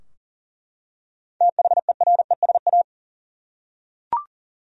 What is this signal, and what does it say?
Morse code, heard as a keyed tone, about 40 wpm, 700 Hz. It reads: THERESA
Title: USA Common Names - 40wpm